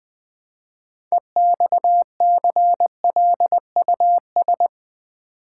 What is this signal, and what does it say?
Morse code, heard as a keyed tone, about 20 wpm, 700 Hz. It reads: EXCLUS
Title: Build A Word - 3 Letters At A Time - 9-Letter Words - 20wpm